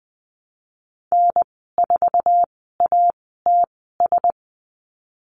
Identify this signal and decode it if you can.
Morse code, heard as a keyed tone, about 20 wpm, spaced as for 15 wpm, 700 Hz. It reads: N4ATS
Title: 055 Lesson 10g - Letter r - Call Signs 20wpm